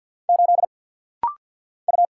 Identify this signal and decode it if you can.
Morse code, heard as a keyed tone, about 50 wpm, 700 Hz. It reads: 8 U
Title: Single Letter-Number - Rapid-Fire 50wpm